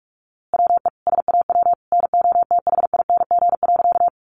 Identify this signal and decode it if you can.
Morse code, heard as a keyed tone, about 44 wpm, 700 Hz. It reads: WE SAW NOTHING.
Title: A Sound Of Thunder 45wpm